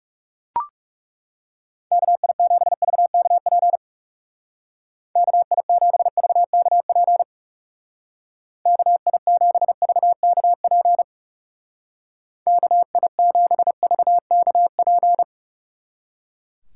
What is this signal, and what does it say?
Morse code, heard as a keyed tone, about 38 wpm, 700 Hz. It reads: KI7VKP KI7VKP KI7VKP KI7VKP